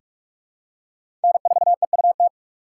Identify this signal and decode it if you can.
Morse code, heard as a keyed tone, about 44 wpm, 700 Hz. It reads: N4EUT
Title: USA Call Signs - 5 Characters - 6x Character Spacing - 45wpm